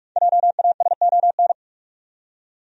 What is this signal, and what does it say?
Morse code, heard as a keyed tone, about 41 wpm, 700 Hz. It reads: JASON
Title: USA Common Names - 45wpm